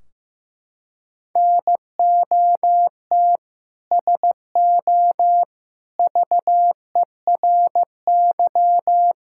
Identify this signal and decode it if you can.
Morse code, heard as a keyed tone, about 15 wpm, 700 Hz. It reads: NOT SO VERY
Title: Sentences from Top 1600 Words 15wpm